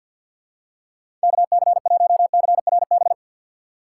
Morse code, heard as a keyed tone, about 53 wpm, 700 Hz. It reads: KC1CRB
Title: USA Call Signs - 6 Characters - 6x Character Spacing - 50wpm